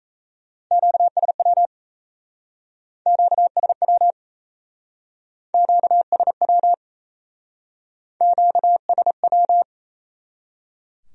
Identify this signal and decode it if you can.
Morse code, heard as a keyed tone, about 37 wpm, 700 Hz. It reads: QSW QSW QSW QSW